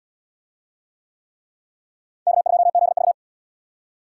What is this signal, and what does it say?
Morse code, heard as a keyed tone, about 68 wpm, 700 Hz. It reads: K2ZV